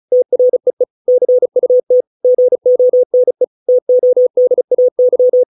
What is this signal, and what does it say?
Morse code, heard as a keyed tone, about 35 wpm, 500 Hz. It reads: TREE CUT GONE TODAY